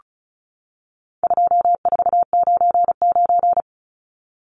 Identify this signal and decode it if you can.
Morse code, heard as a keyed tone, about 34 wpm, 700 Hz. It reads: 2499
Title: Numbers - 4-Digit 35wpm